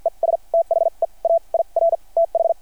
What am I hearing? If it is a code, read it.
Morse code, heard as a keyed tone, about 53 wpm, spaced as for 39 wpm, 660 Hz. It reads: ES THE AIR TH
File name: archived / CRSnet